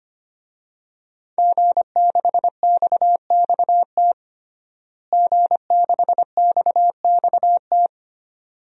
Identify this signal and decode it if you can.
Morse code, heard as a keyed tone, about 25 wpm, 700 Hz. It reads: G6XXT G6XXT
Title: DXCC-100 Call Signs - 5 Characters - Repeated 2x - 25wpm